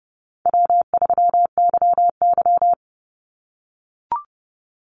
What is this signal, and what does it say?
Morse code, heard as a keyed tone, about 30 wpm, 700 Hz. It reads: W3YY
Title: Easy Call Signs - 30wpm